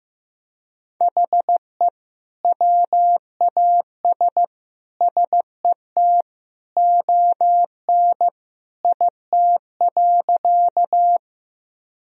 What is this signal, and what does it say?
Morse code, heard as a keyed tone, about 15 wpm, 700 Hz. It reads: HE WAS SET ON IT.
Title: Sentences from Top 1000 Words - Review Part 1 - 15wpm